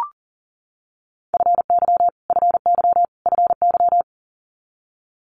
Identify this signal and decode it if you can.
Morse code, heard as a keyed tone, about 40 wpm, 700 Hz. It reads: FY FY FY
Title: Multi-letter Phonemes Repeated 3x 40wpm